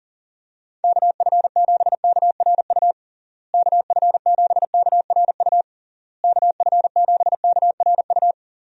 Morse code, heard as a keyed tone, about 40 wpm, 700 Hz. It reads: KF7KRU KF7KRU KF7KRU
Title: USA Call Signs - 6 Characters - Repeated 3x - 40wpm